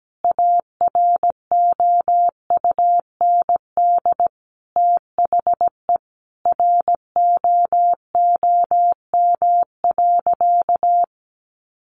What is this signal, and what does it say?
Morse code, heard as a keyed tone, about 17 wpm, 700 Hz. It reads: AROUND THE ROOM.
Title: Sentences from Top 600 Words 17wpm